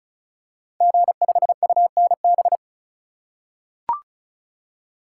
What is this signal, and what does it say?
Morse code, heard as a keyed tone, about 34 wpm, 700 Hz. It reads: G5UNB
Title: Code Groups - 5 Letter:Numbers - 35wpm